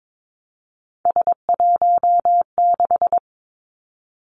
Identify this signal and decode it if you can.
Morse code, heard as a keyed tone, about 22 wpm, 700 Hz. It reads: S16
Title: Three Challenging Letter-Number Combinations 22wpm